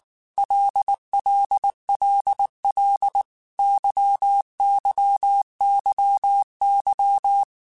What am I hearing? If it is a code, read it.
Morse code, heard as a keyed tone, about 19 wpm, 800 Hz. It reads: LLLL YYYY
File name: Groups@18 / Groups@18 mp3 / LFCYQ@18